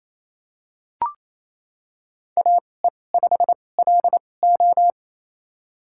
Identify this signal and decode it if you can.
Morse code, heard as a keyed tone, about 28 wpm, spaced as for 20 wpm, 700 Hz. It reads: AE5LO
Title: USA Call Signs - 5 Characters - 2x Character Spacing - 28wpm